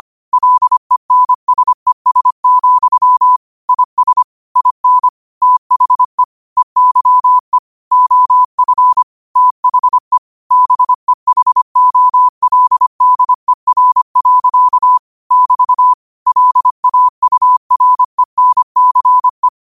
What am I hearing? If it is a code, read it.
Morse code, heard as a keyed tone, about 25 wpm, 1000 Hz. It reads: LENSES, IS IN THE EYE OF THE BEHOLDER. <BT> LAURENCE
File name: qotdfiles / 25wpm